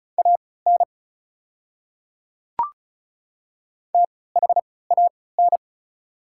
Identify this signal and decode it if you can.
Morse code, heard as a keyed tone, about 36 wpm, spaced as for 20 wpm, 700 Hz. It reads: AN THAN
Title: 3x Character Spacing - Top 200 Words - 35wpm